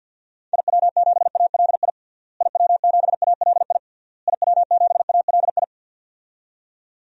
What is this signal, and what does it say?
Morse code, heard as a keyed tone, about 52 wpm, 700 Hz. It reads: IW7ALI IW7ALI IW7ALI
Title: DXCC-100 Call Signs - 6 Characters - Repeated 3x - 50wpm